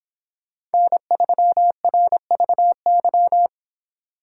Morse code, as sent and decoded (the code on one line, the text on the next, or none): -. ...-- .-. ...- -.--
N3RVY